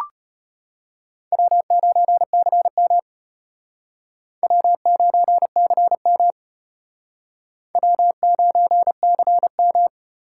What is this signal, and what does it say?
Morse code, heard as a keyed tone, about 31 wpm, 700 Hz. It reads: W9CM W9CM W9CM
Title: Speed Racing - USA 1x2 Call Signs 25wpm